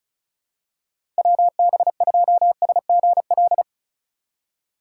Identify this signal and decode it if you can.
Morse code, heard as a keyed tone, about 36 wpm, 700 Hz. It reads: WB2SGL
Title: USA Call Signs - 6 Characters - 4x Character Spacing - 35wpm